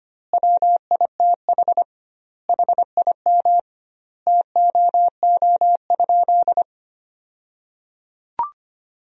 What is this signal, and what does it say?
Morse code, heard as a keyed tone, about 25 wpm, 700 Hz. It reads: WITH HIM TOO?
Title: 2x Word-Spacing - Sentences from Top 500 Words - 25wpm